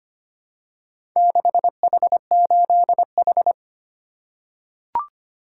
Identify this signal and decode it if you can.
Morse code, heard as a keyed tone, about 25 wpm, 700 Hz. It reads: 6H8H
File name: practice / 25wpm / Four Letter-Number Combinations - 25wpm